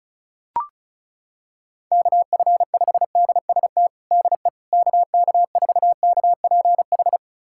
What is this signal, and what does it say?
Morse code, heard as a keyed tone, about 34 wpm, 700 Hz. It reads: KF5DST DE KK4KPH
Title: USA Callsign DE Callsign 35wpm